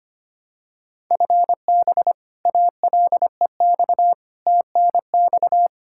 Morse code, heard as a keyed tone, about 25 wpm, 700 Hz. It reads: FB ALEX TNX
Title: Practice QSO - Exchange 3 - More Info - 25wpm